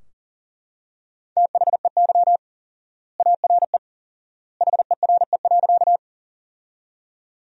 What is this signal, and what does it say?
Morse code, heard as a keyed tone, about 40 wpm, 700 Hz. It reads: THEY ARE HERE.
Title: 4x Word-Spacing - Sentences from Top 200 Words - 40wpm